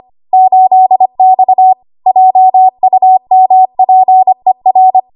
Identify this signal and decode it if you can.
Morse code, heard as a keyed tone, about 25 wpm, 750 Hz. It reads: 8X JUMPER
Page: W1AW Web Code Practice Files